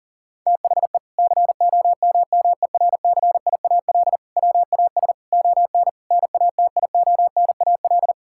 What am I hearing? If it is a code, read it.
Morse code, heard as a keyed tone, about 40 wpm, 700 Hz. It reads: THE COMMERCIAL WAS ON NATIONAL
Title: Sentences from Top 1000 Words - Review Part 5 - 40wpm